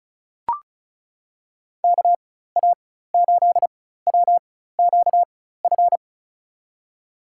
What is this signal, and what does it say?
Morse code, heard as a keyed tone, about 36 wpm, spaced as for 16 wpm, 700 Hz. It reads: KA8WQF
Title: USA Call Signs - 6 Characters - 4x Character Spacing - 35wpm